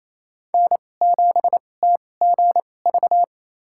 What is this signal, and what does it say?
Morse code, heard as a keyed tone, about 28 wpm, spaced as for 20 wpm, 700 Hz. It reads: N7TGV
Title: USA Call Signs - 5 Characters - 2x Character Spacing - 28wpm